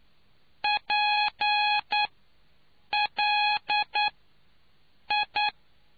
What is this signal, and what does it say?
Morse code, heard as a keyed tone, about 9 wpm, spaced as for 6 wpm, 840 Hz. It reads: PLI